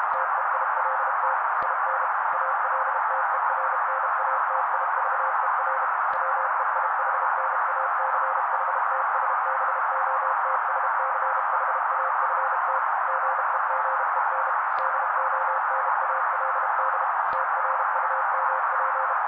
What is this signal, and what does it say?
Morse code, heard as a keyed tone, about 32 wpm, 530 Hz. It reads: TEILT IN ALTERNATIVER WEIHNACHTSBOTSCHAFT GEGEN DONALD TRUMP
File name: daily / 31wpm noise